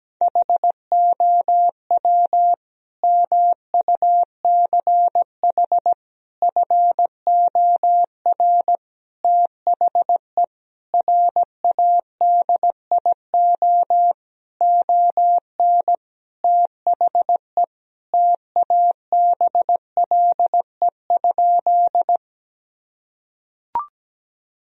Morse code, heard as a keyed tone, about 17 wpm, 700 Hz. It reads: HOW MUCH FOR THE RADIO ON THE TABLE?
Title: Sentences from Top 1000 Words - Review Part 3 - 17wpm